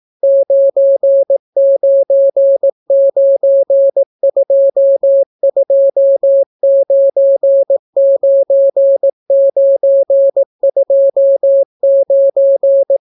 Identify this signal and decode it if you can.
Morse code, heard as a keyed tone, about 18 wpm, 550 Hz. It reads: 9992299929